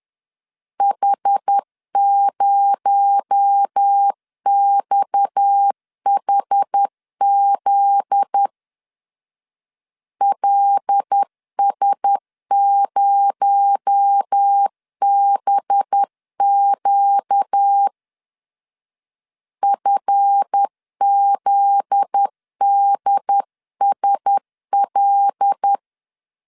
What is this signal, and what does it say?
Morse code, heard as a keyed tone, about 10 wpm, 800 Hz. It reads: H0XHZ LS0BQ FZDSL
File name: LEZIONI / 7 GF9Z / 7 GF9Z 10wpm 2611